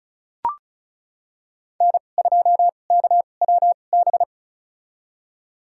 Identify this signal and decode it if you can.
Morse code, heard as a keyed tone, about 36 wpm, spaced as for 25 wpm, 700 Hz. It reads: N2KWB
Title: USA Call Signs - 5 Characters - 2x Character Spacing - 35wpm